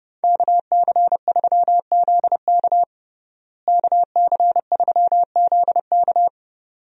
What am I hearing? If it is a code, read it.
Morse code, heard as a keyed tone, about 30 wpm, 700 Hz. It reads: KC3ZK KC3ZK
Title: USA Call Signs - 5 Characters - Repeated 2x - 30wpm